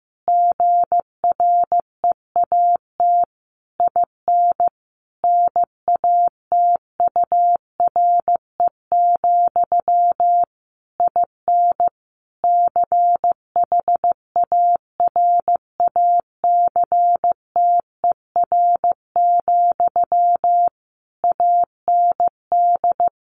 Morse code, heard as a keyed tone, about 15 wpm, 700 Hz. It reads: GREAT IN NATURE, IN CHARACTER, AND
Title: Sentences from Top 1400 Words 15wpm